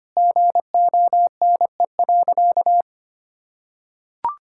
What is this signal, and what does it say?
Morse code, heard as a keyed tone, about 25 wpm, 700 Hz. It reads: GONE.
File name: practice / 25wpm / Sentences with 3-Word Phrases - 25wpm